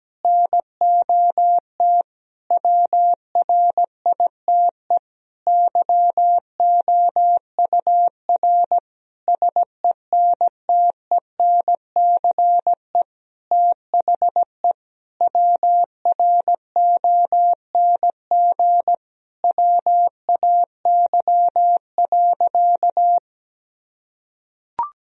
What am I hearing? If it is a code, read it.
Morse code, heard as a keyed tone, about 17 wpm, 700 Hz. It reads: NOT WRITE YOUR SENTENCE THE WRONG WAY.
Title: Sentences from Top 1000 Words - Review Part 1 - 17wpm